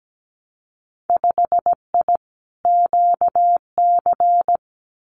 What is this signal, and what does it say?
Morse code, heard as a keyed tone, about 17 wpm, 700 Hz. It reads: 5I QC
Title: Winter Field Day Exchanges Encore 17wpm